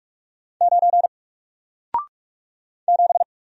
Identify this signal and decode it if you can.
Morse code, heard as a keyed tone, about 44 wpm, 700 Hz. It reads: 9 7
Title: Single Letter-Number Alt 45wpm